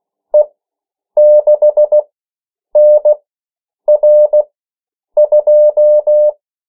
Morse code, heard as a keyed tone, about 16 wpm, spaced as for 8 wpm, 600 Hz. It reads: E6NR2
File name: morse mp3 / 14 urok 13